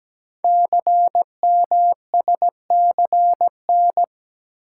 Morse code, heard as a keyed tone, about 17 wpm, 700 Hz. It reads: CMSCN
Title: Code Groups - 5 Letters - 17wpm